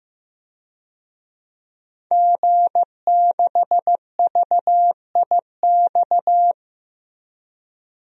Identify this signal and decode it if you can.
Morse code, heard as a keyed tone, about 15 wpm, 700 Hz. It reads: G6VIX
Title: DXCC-100 Call Signs - 5 Characters - Repeated 2x - 15wpm